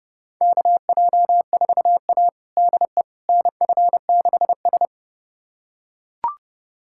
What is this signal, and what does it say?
Morse code, heard as a keyed tone, about 30 wpm, 700 Hz. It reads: KJ4A DE NF6S
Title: USA Callsign DE Callsign - 4 Characters - 30wpm